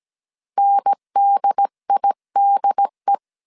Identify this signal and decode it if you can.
Morse code, heard as a keyed tone, about 17 wpm, 800 Hz. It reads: NDIDE
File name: CW / DET / 1 3 1500C S A 16wpm 2141